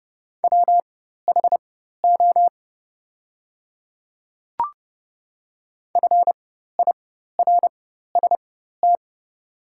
Morse code, heard as a keyed tone, about 30 wpm, spaced as for 14 wpm, 700 Hz. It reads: WHO FIRST
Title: 4x Character Spacing - Top 300 Words - 30wpm